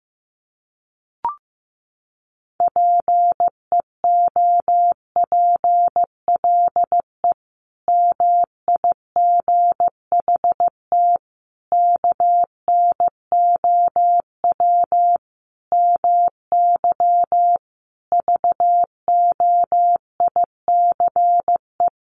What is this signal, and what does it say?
Morse code, heard as a keyed tone, about 15 wpm, 700 Hz. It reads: PEOPLE MIGHT KNOW MY VOICE
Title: Sentences from Top 500 Words 15wpm